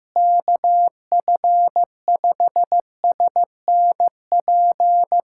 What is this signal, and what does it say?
Morse code, heard as a keyed tone, about 16 wpm, 700 Hz. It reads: KF5SNP
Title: USA Callsign DE Callsign - 6 Characters - 15wpm